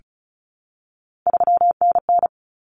Morse code, heard as a keyed tone, about 34 wpm, 700 Hz. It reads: 3NN
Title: Common Cut Numbers - 3-Digit 35wpm